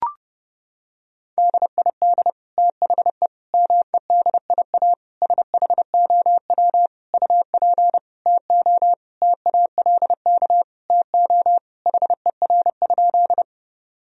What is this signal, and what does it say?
Morse code, heard as a keyed tone, about 30 wpm, 700 Hz. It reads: DID THE MEDIA SHOW UP TO TALK TO HER?
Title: Sentences from Top 1000 Words - Review Part 4 - 30wpm